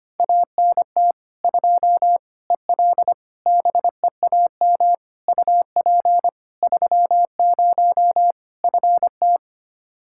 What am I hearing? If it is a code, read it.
Morse code, heard as a keyed tone, about 25 wpm, 700 Hz. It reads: ANT 2 EL BEAM UP 30 FT
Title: K1USN SST QSO Practice 25wpm